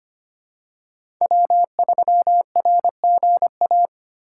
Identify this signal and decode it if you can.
Morse code, heard as a keyed tone, about 25 wpm, 700 Hz. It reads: W3RGA
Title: K1USN SST QSO Practice - 25wpm